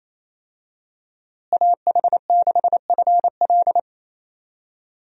Morse code, heard as a keyed tone, about 28 wpm, 700 Hz. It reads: AH6FL
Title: USA Call Signs - 5 Characters - Repeated 3x - 28wpm